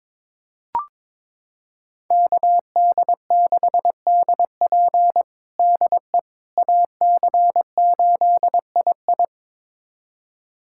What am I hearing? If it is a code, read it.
Morse code, heard as a keyed tone, about 21 wpm, 700 Hz. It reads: KD6DP DE AC8II
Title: USA Callsign DE Callsign - 5 Characters - 22wpm